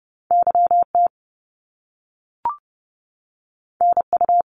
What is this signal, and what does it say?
Morse code, heard as a keyed tone, about 30 wpm, 700 Hz. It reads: YT NU
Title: Canadian Provinces and Territories 30wpm